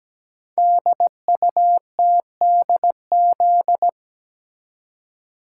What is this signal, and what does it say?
Morse code, heard as a keyed tone, about 17 wpm, 700 Hz. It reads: DUTDZ
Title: Code Groups - 5 Letters - 17wpm